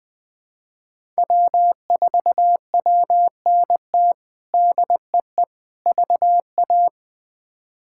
Morse code, heard as a keyed tone, about 20 wpm, 700 Hz. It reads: W4WNT DEE VA
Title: K1USN SST QSO Practice - 20wpm